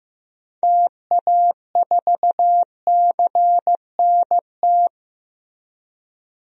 Morse code, heard as a keyed tone, about 15 wpm, 700 Hz. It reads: TA4CNT
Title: DXCC-100 Call Signs - 6 Characters - Repeated 2x - 15wpm